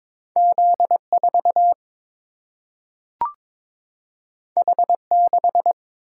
Morse code, heard as a keyed tone, about 22 wpm, 700 Hz. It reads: Z4 H6
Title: Single-Letter Single-Number 22wpm